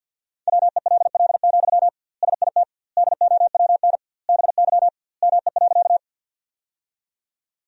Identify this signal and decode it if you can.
Morse code, heard as a keyed tone, about 51 wpm, 700 Hz. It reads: WELL, SIT DOWN BY ME.
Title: 2x Word-Spacing - Sentences from Top 500 Words - 50wpm